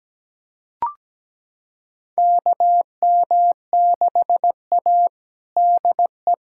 Morse code, heard as a keyed tone, about 17 wpm, 700 Hz. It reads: KM6A DE
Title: USA Callsign DE Callsign - 4 Characters - 17wpm